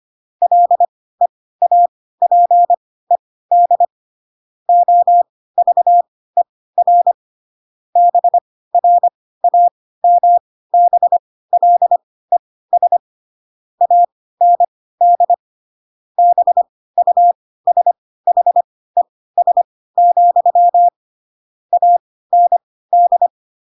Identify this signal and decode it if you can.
Morse code, heard as a keyed tone, about 25 wpm, spaced as for 16 wpm, 700 Hz. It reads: LEAPED OVER BRAMBLES AND BUSHES, AND
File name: Pinocchio / Pinocchio ch4 10 25 ik1hge